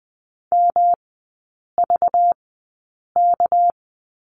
Morse code, heard as a keyed tone, about 20 wpm, spaced as for 8 wpm, 700 Hz. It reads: MVK
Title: 198 Lesson 32e - Letter k - Three Characters 20wpm